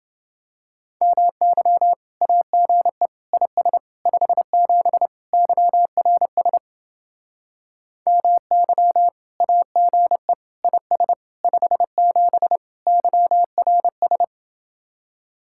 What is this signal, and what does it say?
Morse code, heard as a keyed tone, about 29 wpm, 700 Hz. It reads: MY AGE IS 57 YRS MY AGE IS 57 YRS